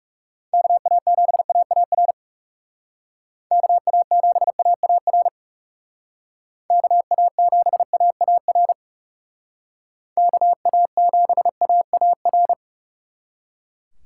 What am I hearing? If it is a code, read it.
Morse code, heard as a keyed tone, about 37 wpm, 700 Hz. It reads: KA7AAR KA7AAR KA7AAR KA7AAR
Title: Speed Racing - USA 2x3 Call Signs 30wpm